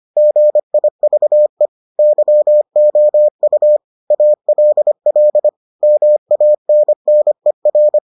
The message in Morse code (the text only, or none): GIVE YOU ALL MANNER